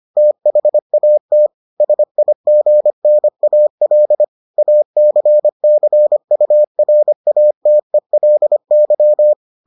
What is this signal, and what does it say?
Morse code, heard as a keyed tone, about 25 wpm, 600 Hz. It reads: THAT SIGNAL ACCURATELY